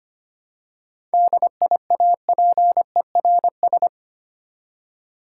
Morse code, heard as a keyed tone, about 25 wpm, 690 Hz. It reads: DIAPERS
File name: practice / 25wpm / Easy Words with Hints - 25wpm